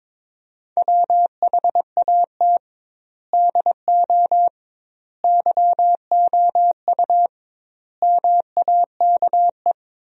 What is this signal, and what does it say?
Morse code, heard as a keyed tone, about 22 wpm, 700 Hz. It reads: WHAT DO YOU MAKE
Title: 2x Word-Spacing - Sentences from Top 100 Words - 22wpm